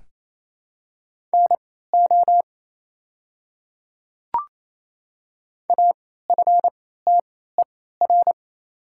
Morse code, heard as a keyed tone, about 28 wpm, spaced as for 16 wpm, 700 Hz. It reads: NO AFTER
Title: 3x Character Spacing - Top 200 Words - 28wpm